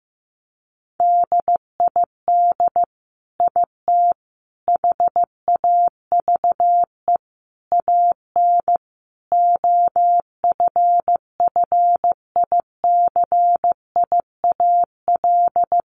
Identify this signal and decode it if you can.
Morse code, heard as a keyed tone, about 15 wpm, 700 Hz. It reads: DID IT HAVE AN OFFICIAL
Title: Sentences from Top 1200 Words 15wpm